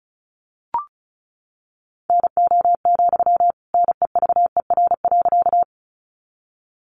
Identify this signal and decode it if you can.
Morse code, heard as a keyed tone, about 34 wpm, 700 Hz. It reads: NO, NEVER.
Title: Sentences from Top 300 Words 35wpm